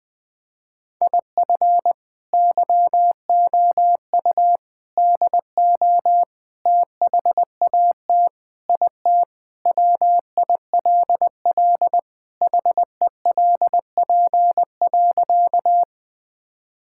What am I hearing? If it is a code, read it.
Morse code, heard as a keyed tone, about 21 wpm, 700 Hz. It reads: IF YOU DO THAT IT WILL HELP.
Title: Sentences with 3-Word Phrases - 20wpm